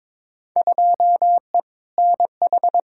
Morse code, heard as a keyed tone, about 22 wpm, 700 Hz. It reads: 2E NH
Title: Field Day Exchanges - 22wpm